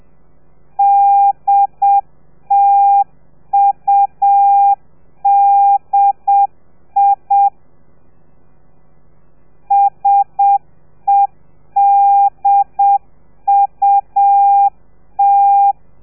Morse code, heard as a keyed tone, about 7 wpm, 800 Hz. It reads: DTUDI SEDUT